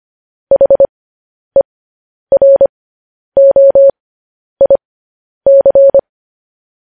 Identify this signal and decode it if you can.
Morse code, heard as a keyed tone, about 25 wpm, spaced as for 10 wpm, 550 Hz. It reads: HEROIC
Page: News Headlines in Morse Code at 10 WPM